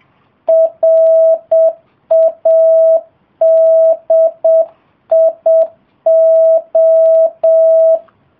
Morse code, heard as a keyed tone, about 7 wpm, 650 Hz. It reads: RADIO